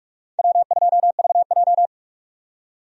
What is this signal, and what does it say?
Morse code, heard as a keyed tone, about 42 wpm, 700 Hz. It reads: W2VJ
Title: USA Call Signs - 4 Characters - 45wpm